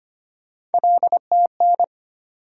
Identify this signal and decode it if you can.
Morse code, heard as a keyed tone, about 25 wpm, 700 Hz. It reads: LTN
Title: USA Call Sign Postfixes - 3 Letters 25wpm